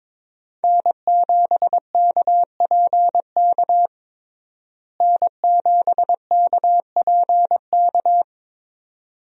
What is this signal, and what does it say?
Morse code, heard as a keyed tone, about 22 wpm, 700 Hz. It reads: N7KPK N7KPK
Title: USA Call Signs - 5 Characters - Repeated 2x - 22wpm